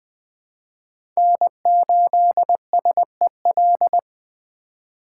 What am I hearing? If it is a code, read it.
Morse code, heard as a keyed tone, about 20 wpm, 700 Hz. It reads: N8SEL
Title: USA Call Signs - 5 Characters - 4x Character Spacing - 20wpm